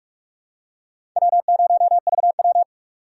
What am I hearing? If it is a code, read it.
Morse code, heard as a keyed tone, about 46 wpm, 700 Hz. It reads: W0VW
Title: USA Call Signs - 4 Characters - 2x Character Spacing - 45wpm